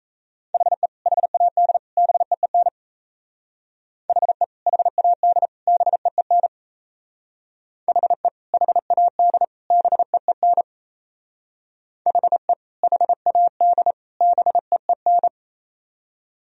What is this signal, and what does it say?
Morse code, heard as a keyed tone, about 38 wpm, 700 Hz. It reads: HE HAD BEEN HE HAD BEEN HE HAD BEEN HE HAD BEEN